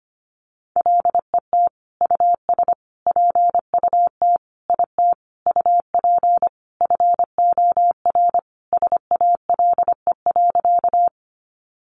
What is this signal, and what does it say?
Morse code, heard as a keyed tone, about 25 wpm, 700 Hz. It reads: LET US PUT IT UP FOR SALE.
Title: Sentences from Top 1400 Words 25wpm